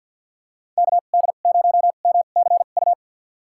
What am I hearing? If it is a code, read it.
Morse code, heard as a keyed tone, about 53 wpm, spaced as for 37 wpm, 700 Hz. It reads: KD0MCU
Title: USA Call Signs - 2x Character Spacing - 50wpm